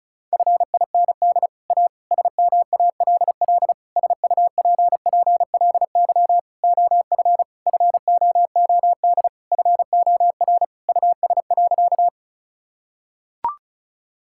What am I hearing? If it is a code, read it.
Morse code, heard as a keyed tone, about 36 wpm, 700 Hz. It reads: FIND A SMALL SUPPLY OF FOOD FOR US.
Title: Sentences from Top 1000 Words - Review Part 5 - 35wpm